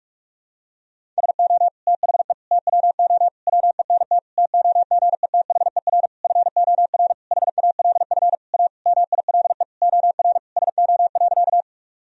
Morse code, heard as a keyed tone, about 44 wpm, 700 Hz. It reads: SO THE TWO WENT TOGETHER FOR HALF A MILE OR SO.
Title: Sentences from Top 1000 Words - Review Part 1 - 45wpm